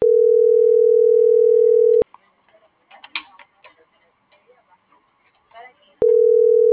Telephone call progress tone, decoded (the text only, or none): ringback tone